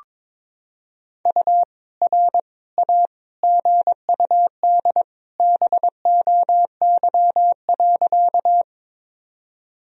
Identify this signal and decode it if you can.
Morse code, heard as a keyed tone, about 22 wpm, 700 Hz. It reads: U R A GUD BOY.
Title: Sentences from Top 200 Words and Abbreviations 22wpm